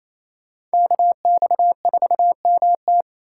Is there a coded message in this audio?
Yes: Morse code from 0.7 to 3.0 s, heard as a keyed tone, about 35 wpm, 700 Hz. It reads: KX4MT